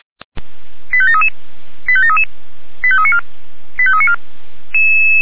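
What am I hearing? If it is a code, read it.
Five-tone selective call (ZVEI), heard as quick runs of five tones, tones of about 70 ms. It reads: 85733 85733 85394 85394
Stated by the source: ZVEI five-tone selective call received via SDR.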